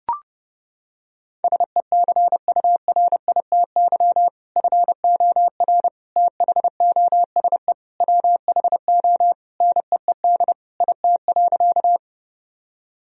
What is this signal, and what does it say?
Morse code, heard as a keyed tone, about 32 wpm, 700 Hz. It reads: SECURITY FOR THOSE WHO NEED IT.
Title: Sentences from Top 1000 Words - Review Part 4 - 30wpm